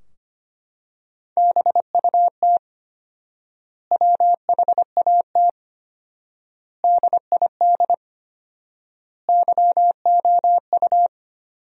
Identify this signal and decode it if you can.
Morse code, heard as a keyed tone, about 25 wpm, 700 Hz. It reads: BUT WHAT DID YOU
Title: 4x Word-Spacing - Sentences from Top 300 Words - 25wpm